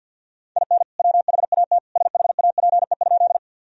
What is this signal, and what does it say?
Morse code, heard as a keyed tone, about 46 wpm, 700 Hz. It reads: IN WHAT SHAPE?
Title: Sentences from Top 1000 Words - Review Part 3 - 50wpm